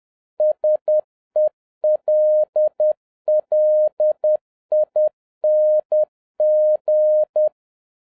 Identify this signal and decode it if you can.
Morse code, heard as a keyed tone, about 10 wpm, 610 Hz. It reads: SELLING